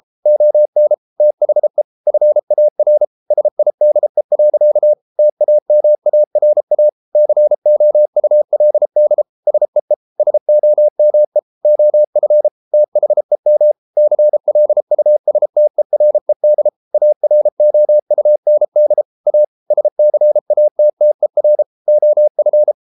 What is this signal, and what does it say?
Morse code, heard as a keyed tone, about 33 wpm, 600 Hz. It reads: ON THE FAR SIDE. TAMARA COULD SEE SOME OF THEM CLUSTERED AROUND A SCATTER OF